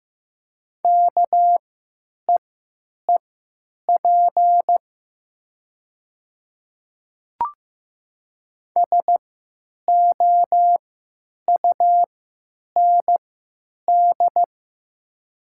Morse code, heard as a keyed tone, about 15 wpm, spaced as for 8 wpm, 700 Hz. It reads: KEEP SOUND
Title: 3x Character Spacing - Top 300 Words - 15wpm